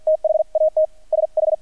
Morse code, heard as a keyed tone, about 48 wpm, spaced as for 41 wpm, 630 Hz. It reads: THAT SH
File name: archived / CRSnet